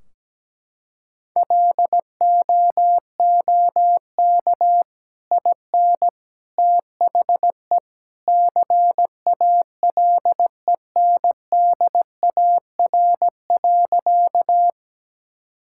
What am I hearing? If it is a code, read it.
Morse code, heard as a keyed tone, about 17 wpm, 700 Hz. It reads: LOOK IN THE CALENDAR.
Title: Sentences from Top 1600 Words 17wpm